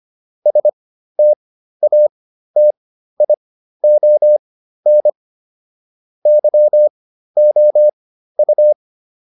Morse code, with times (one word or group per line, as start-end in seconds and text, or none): 0.5-5.1 STATION
6.3-8.7 YOU